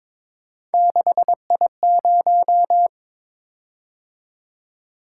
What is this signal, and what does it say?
Morse code, heard as a keyed tone, about 22 wpm, 700 Hz. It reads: 6I0
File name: practice / 22wpm / Three Letter-Number Combinations - 22wpm